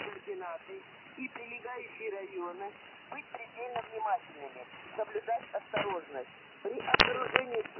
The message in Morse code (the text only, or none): none